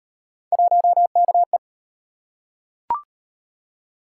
Morse code, heard as a keyed tone, about 38 wpm, 700 Hz. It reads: 1KE